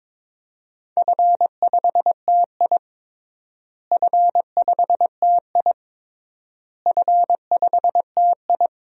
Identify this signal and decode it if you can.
Morse code, heard as a keyed tone, about 22 wpm, 700 Hz. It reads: F5TI F5TI F5TI
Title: DXCC-100 Call Signs - 4 Characters - Repeated 3x - 22wpm